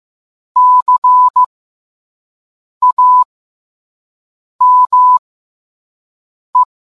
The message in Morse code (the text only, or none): CAME